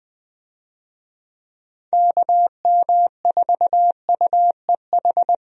Morse code, heard as a keyed tone, about 21 wpm, 700 Hz. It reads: KM4UEH
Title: USA Call Signs - 6 Characters - 20wpm